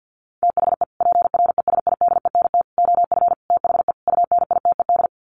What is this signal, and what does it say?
Morse code, heard as a keyed tone, about 50 wpm, 700 Hz. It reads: THE PRESIDENT OF THE UNITED
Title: Sets of 5 Words 50wpm